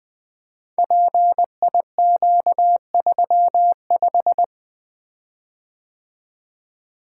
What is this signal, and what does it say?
Morse code, heard as a keyed tone, about 20 wpm, 700 Hz. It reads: PIQ35
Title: Code Groups - 5 Letter:Numbers - 20wpm